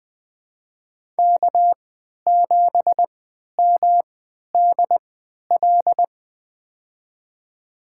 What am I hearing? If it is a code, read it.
Morse code, heard as a keyed tone, about 20 wpm, spaced as for 11 wpm, 700 Hz. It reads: K7MDL
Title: USA Call Signs - 5 Characters - 3x Character Spacing - 20wpm